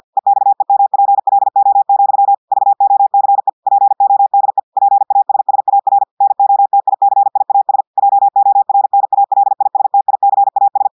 Morse code, heard as a keyed tone, about 50 wpm, 800 Hz. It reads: EXEMPLO, VOCE PODE PASSAR NOTICIAS JORNALISTICAS
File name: portugais texte en clair 2 vitesse 50 mots minutes